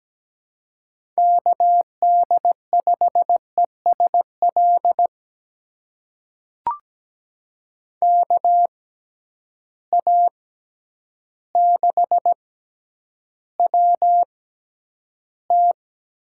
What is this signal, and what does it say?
Morse code, heard as a keyed tone, about 17 wpm, spaced as for 9 wpm, 700 Hz. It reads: KD5ESL KA6WT
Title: USA Call Signs - 6 Characters - 6x Character Spacing - 17wpm